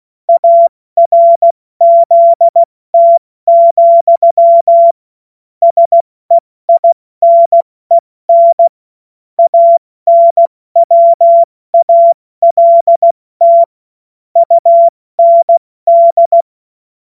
Morse code, heard as a keyed tone, about 16 wpm, 680 Hz. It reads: ARZT, SEINEN ANWALT UND
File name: Morsekurs / Witze / Witz 06. September 2022 70 F80